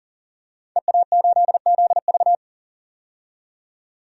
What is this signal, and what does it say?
Morse code, heard as a keyed tone, about 40 wpm, 700 Hz. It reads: EA8ZV
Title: DXCC-100 Call Signs - 5 Characters - 40wpm